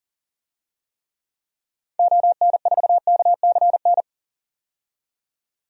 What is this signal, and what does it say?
Morse code, heard as a keyed tone, about 37 wpm, 700 Hz. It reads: ON4KCN
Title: DXCC-100 Call Signs - 6 Characters - 40wpm